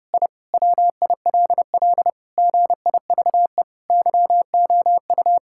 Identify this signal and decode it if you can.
Morse code, heard as a keyed tone, about 32 wpm, 700 Hz. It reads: I WILL GIVE YOU
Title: Sentences from Top 1000 Words - Review Part 6 - 30wpm